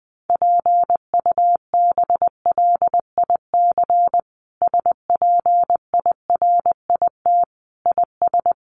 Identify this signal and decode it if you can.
Morse code, heard as a keyed tone, about 20 wpm, 700 Hz. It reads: PUBLIC SPIRIT IS
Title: Sentences from Top 1400 Words 20wpm